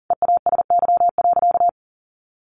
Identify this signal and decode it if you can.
Morse code, heard as a keyed tone, about 40 wpm, 700 Hz. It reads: EASY.
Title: Sentences from Top 700 Words 40wpm